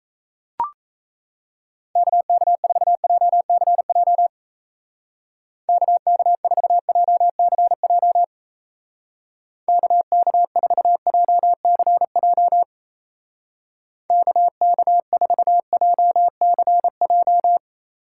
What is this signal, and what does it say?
Morse code, heard as a keyed tone, about 33 wpm, 700 Hz. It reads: KK4JCJ KK4JCJ KK4JCJ KK4JCJ
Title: Speed Racing - USA 2x3 Call Signs 28wpm